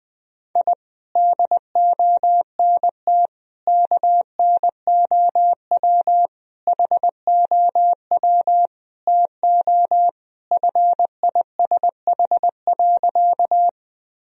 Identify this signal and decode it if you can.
Morse code, heard as a keyed tone, about 20 wpm, 700 Hz. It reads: I DONT KNOW HOW TO FISH.
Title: Sentences from Top 1000 Words - Review Part 1 - 20wpm